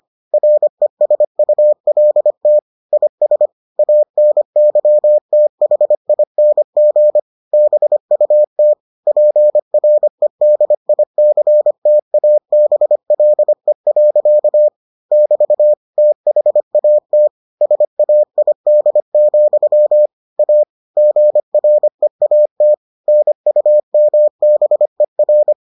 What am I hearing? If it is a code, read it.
Morse code, heard as a keyed tone, about 25 wpm, 600 Hz. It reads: RESULT IS ANYTHING BUT PREDICTABLE. <BT> THAT SAID, A GREAT NUMBER